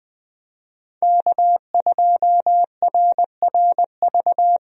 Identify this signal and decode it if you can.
Morse code, heard as a keyed tone, about 20 wpm, 700 Hz. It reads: K2RRV
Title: USA Call Signs - 5 Characters - Repeated 3x - 20wpm